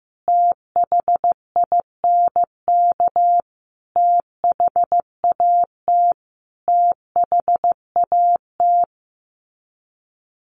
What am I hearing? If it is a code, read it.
Morse code, heard as a keyed tone, about 15 wpm, 700 Hz. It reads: THINK THAT THAT
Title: Sets of 5 Words Encore 15wpm